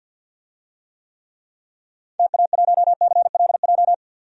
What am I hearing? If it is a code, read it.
Morse code, heard as a keyed tone, about 51 wpm, 700 Hz. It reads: TA1CLJ
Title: DXCC-100 Call Signs - 6 Characters - 50wpm